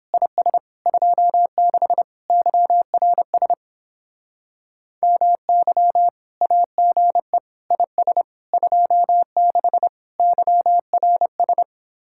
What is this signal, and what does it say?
Morse code, heard as a keyed tone, about 31 wpm, 700 Hz. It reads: IS 26 YRS MY AGE IS 26 YRS